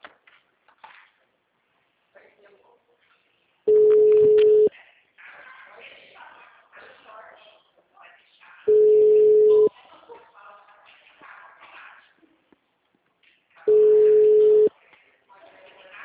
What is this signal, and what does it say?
Ringback tone (Europe, 425 Hz), with speech.